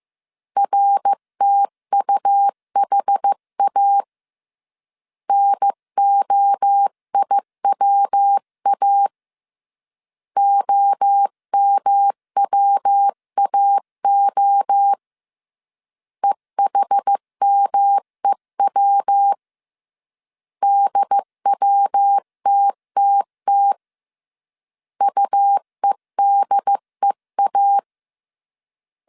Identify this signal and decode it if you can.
Morse code, heard as a keyed tone, about 15 wpm, 800 Hz. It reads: RTUHA NOIWA OMWAO EHMEW DWTTT UEDEA
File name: CW / DET / 1 3 1500C S A 14wpm 2447